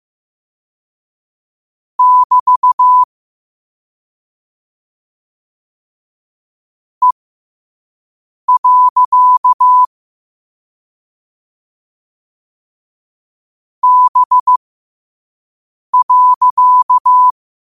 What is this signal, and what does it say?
Morse code, heard as a keyed tone, about 15 wpm, spaced as for 5 wpm, 1000 Hz. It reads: <BT> E. B.
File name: qotdfiles / 5wpm